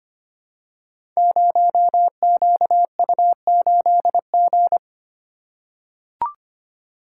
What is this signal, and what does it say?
Morse code, heard as a keyed tone, about 25 wpm, 700 Hz. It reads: 0QU8G
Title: Code Groups - 5 Letter:Numbers - 25wpm